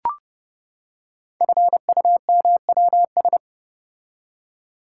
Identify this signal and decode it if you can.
Morse code, heard as a keyed tone, about 30 wpm, 700 Hz. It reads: FUMWS
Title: Code Groups - 5 Letters - 30wpm